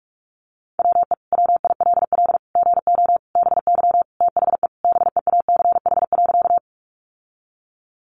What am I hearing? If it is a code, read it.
Morse code, heard as a keyed tone, about 44 wpm, 700 Hz. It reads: WE WILL GO BY THE BEACH.
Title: Sentences from Top 1500 Words 45wpm